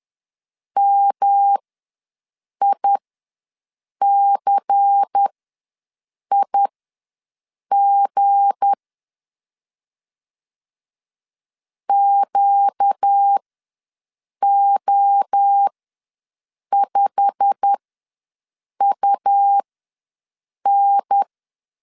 Morse code, heard as a keyed tone, about 10 wpm, spaced as for 6 wpm, 800 Hz. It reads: MICIG QO5UN
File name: LEZIONI / 8 YJ45 / 8 YJ45 ws 10wpm 3843